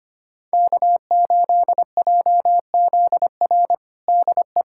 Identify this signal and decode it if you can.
Morse code, heard as a keyed tone, about 25 wpm, 700 Hz. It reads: K8JZR DE
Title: USA Callsign DE Callsign - 5 Characters - 25wpm